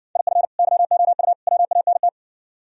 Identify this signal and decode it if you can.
Morse code, heard as a keyed tone, about 67 wpm, 700 Hz. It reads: IF YOU WANT